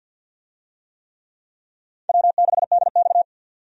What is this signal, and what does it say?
Morse code, heard as a keyed tone, about 48 wpm, 700 Hz. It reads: W6DX
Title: USA Call Signs - 4 Characters - 50wpm